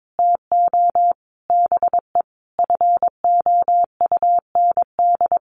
Morse code, heard as a keyed tone, about 22 wpm, 700 Hz. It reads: TO BE FOUND
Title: Sentences from Top 1400 Words 22wpm